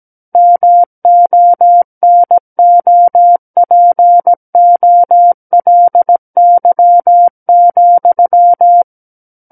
Morse code, heard as a keyed tone, about 17 wpm, 700 Hz. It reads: MONOPOLY,
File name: Book One / 08-Chapter 8 - 17 WPM